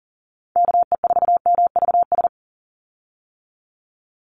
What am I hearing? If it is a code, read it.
Morse code, heard as a keyed tone, about 40 wpm, 700 Hz. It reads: KE4MVS
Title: USA Call Signs 40wpm